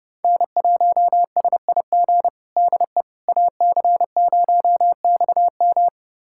Morse code, heard as a keyed tone, about 30 wpm, 700 Hz. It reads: N1SIG DE AC0XM
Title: USA Callsign DE Callsign - 5 Characters - 30wpm